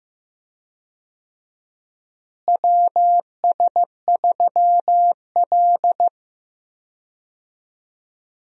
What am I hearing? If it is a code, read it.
Morse code, heard as a keyed tone, about 16 wpm, 700 Hz. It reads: WS3L
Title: USA Call Signs - 4 Characters - Repeated 2x - 15wpm